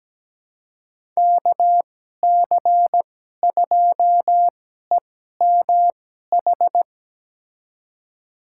Morse code, heard as a keyed tone, about 17 wpm, spaced as for 12 wpm, 700 Hz. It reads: KC2EMH
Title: USA Call Signs - 6 Characters - 2x Character Spacing - 17wpm